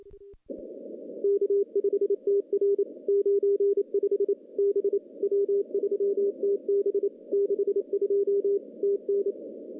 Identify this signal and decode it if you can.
Morse code, heard as a keyed tone, about 28 wpm, 400 Hz. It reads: K5TR 95 B W3TB 62 TN
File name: contests / CW-SS-k5tr- -0200